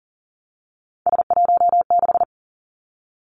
Morse code, heard as a keyed tone, about 38 wpm, 700 Hz. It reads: S16
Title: Three Challenging Letter-Number Combinations 40wpm